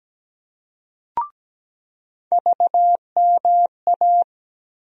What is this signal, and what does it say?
Morse code, heard as a keyed tone, about 17 wpm, 700 Hz. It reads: VMA
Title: USA Call Sign Postfixes - 3 Letters 17wpm